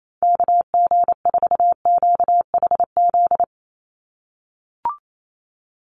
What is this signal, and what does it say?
Morse code, heard as a keyed tone, about 28 wpm, 700 Hz. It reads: KG4QHZ
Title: USA Call Signs 28wpm